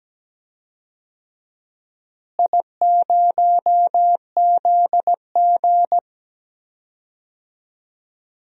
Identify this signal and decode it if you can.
Morse code, heard as a keyed tone, about 17 wpm, 700 Hz. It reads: I0ZG
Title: DXCC-100 Call Signs - 4 Characters - Repeated 3x - 17wpm